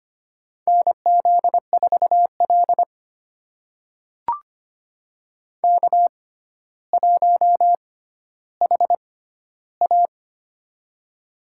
Morse code, heard as a keyed tone, about 25 wpm, spaced as for 13 wpm, 700 Hz. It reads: NZ4L K1HA
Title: USA Call Signs - 4 Characters - 6x Character Spacing - 25wpm